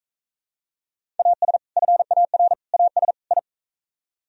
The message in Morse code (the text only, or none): AS FAR AS I